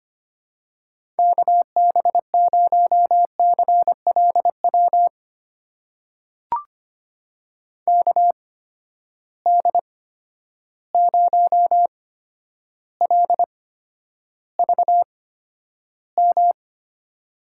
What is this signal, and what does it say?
Morse code, heard as a keyed tone, about 25 wpm, spaced as for 11 wpm, 700 Hz. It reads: KB0CLW KD0LVM
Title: USA Call Signs - 6 Characters - 8x Character Spacing - 25wpm